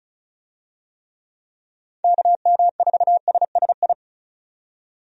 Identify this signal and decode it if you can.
Morse code, heard as a keyed tone, about 34 wpm, 700 Hz. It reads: KM4SSI
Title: USA Call Signs - 6 Characters - 35wpm